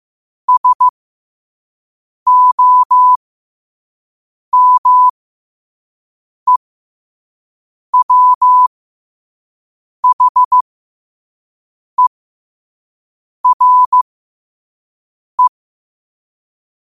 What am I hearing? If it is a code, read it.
Morse code, heard as a keyed tone, about 15 wpm, spaced as for 5 wpm, 1000 Hz. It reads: SOMEWHERE
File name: qotdfiles / 5wpm